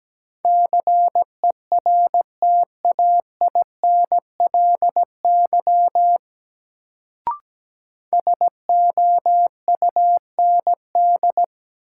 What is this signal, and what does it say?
Morse code, heard as a keyed tone, about 17 wpm, 700 Hz. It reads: CERTAINLY SOUND
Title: Sentences from Top 900 Words - New Word Drill - 17wpm